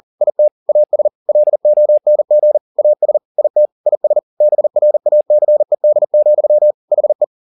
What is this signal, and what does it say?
Morse code, heard as a keyed tone, about 40 wpm, 600 Hz. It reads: IT AS LONG AS IT IS BRACED, HE
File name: the old man and the sea 2 40wpm